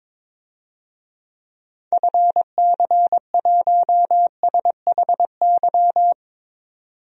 Morse code, heard as a keyed tone, about 22 wpm, 700 Hz. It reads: FC1SHY
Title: DXCC-100 Call Signs - 6 Characters - 22wpm